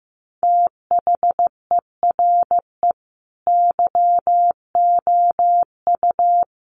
Morse code, heard as a keyed tone, about 15 wpm, 700 Hz. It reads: THERE YOU
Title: Sentences from Top 100 Words 15wpm